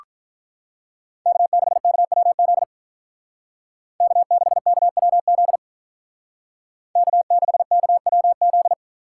Morse code, heard as a keyed tone, about 46 wpm, 700 Hz. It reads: K6KWZ K6KWZ K6KWZ